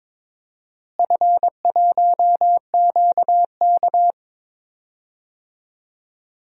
Morse code, heard as a keyed tone, about 22 wpm, 700 Hz. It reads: F1QK
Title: DXCC-100 Call Signs - 4 Characters - Repeated 2x - 22wpm